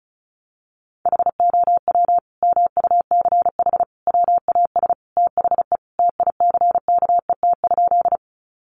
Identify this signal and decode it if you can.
Morse code, heard as a keyed tone, about 34 wpm, 700 Hz. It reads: HOW MUCH WAS THE TICKET?
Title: Sentences from Top 1400 Words 35wpm